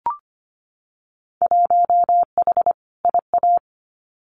Morse code, heard as a keyed tone, about 25 wpm, 700 Hz. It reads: 1H IA
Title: Winter Field Day Exchanges 25wpm